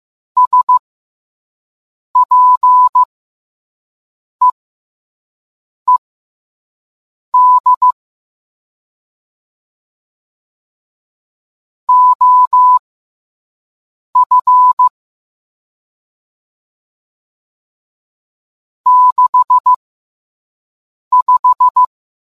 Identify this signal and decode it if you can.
Morse code, heard as a keyed tone, about 15 wpm, spaced as for 5 wpm, 1000 Hz. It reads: SPEED OF 65